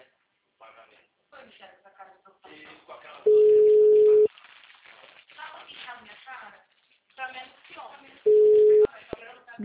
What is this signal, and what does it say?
Ringback tone (Europe, 425 Hz), with speech.